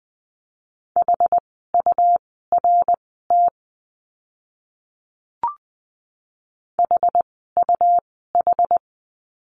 Morse code, heard as a keyed tone, about 20 wpm, spaced as for 14 wpm, 700 Hz. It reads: HURT HUH
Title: 101 Lesson 17f - Letter u - Words 20wpm